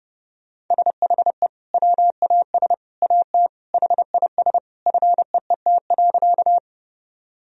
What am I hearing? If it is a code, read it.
Morse code, heard as a keyed tone, about 30 wpm, 700 Hz. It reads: SHE WAS AT HIS FEET.
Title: Sentences from Top 1000 Words - Review Part 1 - 30wpm